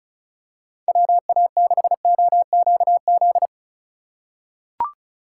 Morse code, heard as a keyed tone, about 36 wpm, 700 Hz. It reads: WA6OQZ
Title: USA Call Signs - 2x Character Spacing - 35wpm